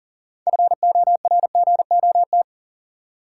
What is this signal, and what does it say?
Morse code, heard as a keyed tone, about 37 wpm, 700 Hz. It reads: FORGOT
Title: Sentences from Top 1400 Words - New Word Drill Alt - 40wpm